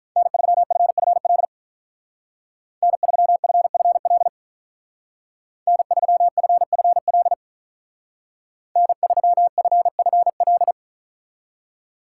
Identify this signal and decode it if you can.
Morse code, heard as a keyed tone, about 43 wpm, 700 Hz. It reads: N3FFL N3FFL N3FFL N3FFL
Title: Speed Racing - USA 1x3 Call Signs 35wpm